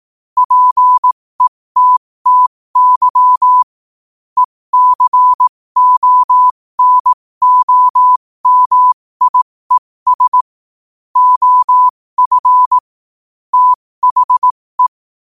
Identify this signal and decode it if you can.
Morse code, heard as a keyed tone, about 18 wpm, spaced as for 15 wpm, 1000 Hz. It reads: PETTY ECONOMIES OF THE
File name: qotdfiles / 15wpm